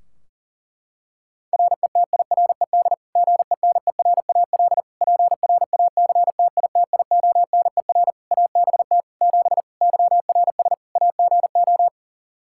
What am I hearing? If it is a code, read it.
Morse code, heard as a keyed tone, about 40 wpm, 700 Hz. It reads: RETIRED GENERAL PRACTITIONER ABT 7 YRS AGO
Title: Practice QSO - Exchange 3 - More Info - 40wpm